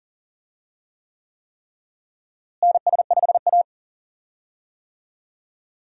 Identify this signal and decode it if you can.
Morse code, heard as a keyed tone, about 40 wpm, 700 Hz. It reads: NS5A
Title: USA Call Signs - 4 Characters - Repeated 3x - 40wpm